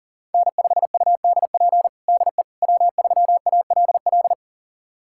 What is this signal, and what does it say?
Morse code, heard as a keyed tone, about 40 wpm, 700 Hz. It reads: N5UDP DE W3ALL
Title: USA Callsign DE Callsign - 5 Characters - 40wpm